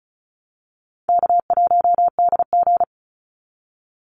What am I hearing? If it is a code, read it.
Morse code, heard as a keyed tone, about 34 wpm, 700 Hz. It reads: K1DG
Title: Call Signs Repeated 2x - 35wpm